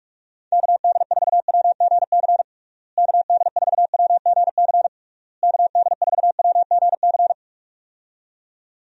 Morse code, heard as a keyed tone, about 48 wpm, 700 Hz. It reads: KD4WGC KD4WGC KD4WGC
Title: USA Call Signs - 6 Characters - Repeated 3x - 45wpm